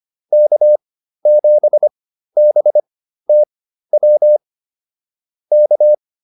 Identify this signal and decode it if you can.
Morse code, heard as a keyed tone, about 25 wpm, spaced as for 13 wpm, 600 Hz. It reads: K7BTW K